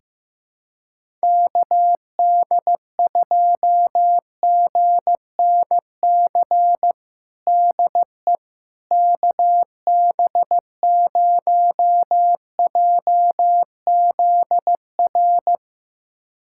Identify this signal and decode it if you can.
Morse code, heard as a keyed tone, about 15 wpm, 700 Hz. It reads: KD2GNC DE KB0JZR
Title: USA Callsign DE Callsign - 6 Characters - 15wpm